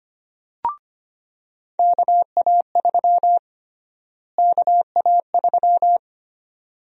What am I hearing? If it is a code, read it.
Morse code, heard as a keyed tone, about 25 wpm, 700 Hz. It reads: KA3 KA3
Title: USA Call Sign Prefixes - Repeated 2x - 25wpm